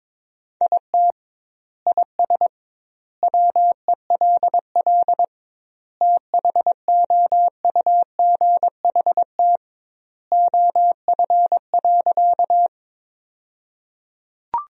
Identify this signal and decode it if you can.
Morse code, heard as a keyed tone, about 22 wpm, 700 Hz. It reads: IT IS WELL THOUGHT OF.
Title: 2x Word-Spacing - Sentences from Top 300 Words - 22wpm